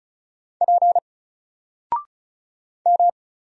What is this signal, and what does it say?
Morse code, heard as a keyed tone, about 36 wpm, 700 Hz. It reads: P M
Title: Single Letter-Number Alt 35wpm